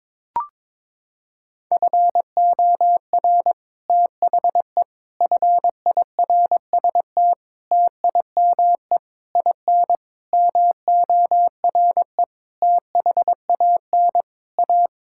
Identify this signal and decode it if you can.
Morse code, heard as a keyed tone, about 22 wpm, 700 Hz. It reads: FOR THE FIRST TIME IN MORE THAN A
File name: Sets of 9 Words - 22wpm